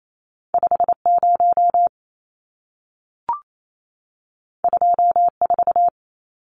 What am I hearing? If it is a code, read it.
Morse code, heard as a keyed tone, about 28 wpm, 700 Hz. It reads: 50 24
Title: Numbers - 2-Digit 28wpm